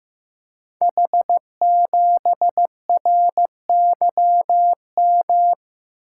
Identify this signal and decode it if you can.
Morse code, heard as a keyed tone, about 15 wpm, 700 Hz. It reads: H7RYM
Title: Code Groups - 5 Letter:Numbers - 15wpm